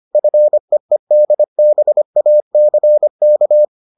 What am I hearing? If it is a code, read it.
Morse code, heard as a keyed tone, about 25 wpm, 600 Hz. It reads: FEEDBACK